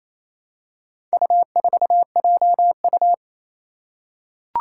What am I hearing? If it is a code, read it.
Morse code, heard as a keyed tone, about 27 wpm, 700 Hz. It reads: U4JU
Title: Four Challenging Letter-Number Combinations 28wpm